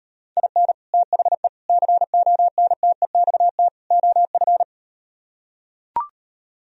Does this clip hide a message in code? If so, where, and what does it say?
Yes: Morse code from 0.4 to 4.6 s, heard as a keyed tone, about 38 wpm, 700 Hz. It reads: IN THE CONTEXT OF